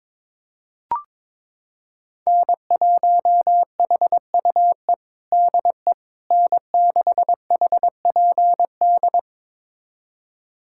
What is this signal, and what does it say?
Morse code, heard as a keyed tone, about 22 wpm, 700 Hz. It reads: N1HUE DE N6HPD
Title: USA Callsign DE Callsign 22wpm